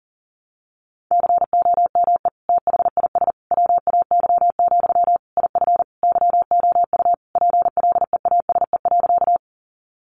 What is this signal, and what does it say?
Morse code, heard as a keyed tone, about 40 wpm, 700 Hz. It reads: COME THIS WAY, IF YOU PLEASE.
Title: Sentences from Top 1500 Words 40wpm